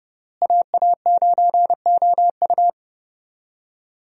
Morse code, heard as a keyed tone, about 30 wpm, 700 Hz. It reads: AA9OU
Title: USA Call Signs - 5 Characters - Repeated 3x - 30wpm